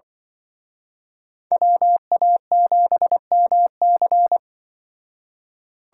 Morse code, heard as a keyed tone, about 25 wpm, 700 Hz. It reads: WA7MC
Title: Speed Racing - USA 2x2 Call Signs 20wpm